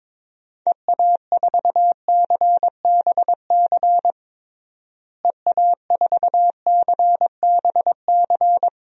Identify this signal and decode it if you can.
Morse code, heard as a keyed tone, about 22 wpm, 700 Hz. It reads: EA4CBC EA4CBC
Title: DXCC-100 Call Signs - 6 Characters - Repeated 2x - 22wpm